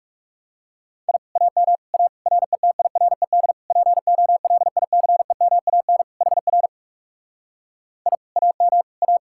Morse code, heard as a keyed tone, about 38 wpm, 700 Hz. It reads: I AM A RETIRED POLICEMAN HR I AM A
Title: Speed-Racing Common QSO Phrases Encore - 30wpm